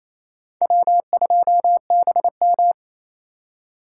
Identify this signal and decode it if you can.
Morse code, heard as a keyed tone, about 35 wpm, 700 Hz. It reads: W2BM
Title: Speed Racing - USA 1x2 Call Signs 28wpm